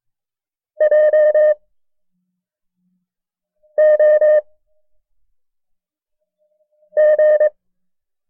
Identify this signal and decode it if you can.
Morse code, heard as a keyed tone, about 23 wpm, spaced as for 4 wpm, 610 Hz. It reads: JOG
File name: FreeFiles / CW-practice / KB6MT-tape1-side2